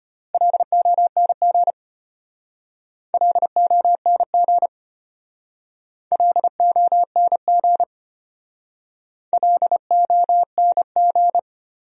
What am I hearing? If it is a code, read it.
Morse code, heard as a keyed tone, about 32 wpm, 700 Hz. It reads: LONG LONG LONG LONG